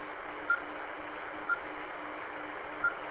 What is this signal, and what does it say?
There is no code.